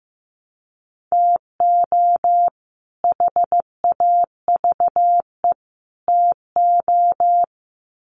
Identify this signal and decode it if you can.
Morse code, heard as a keyed tone, about 15 wpm, 700 Hz. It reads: TO HAVE TO
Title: Sets of 3 Words 15wpm